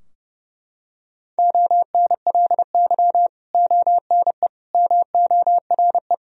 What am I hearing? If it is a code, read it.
Morse code, heard as a keyed tone, about 30 wpm, 700 Hz. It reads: ONLY ONE MORE
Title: Sentences from Top 1000 Words - Review Part 2 - 30wpm